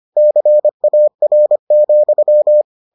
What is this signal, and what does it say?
Morse code, heard as a keyed tone, about 25 wpm, 600 Hz. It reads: CAR,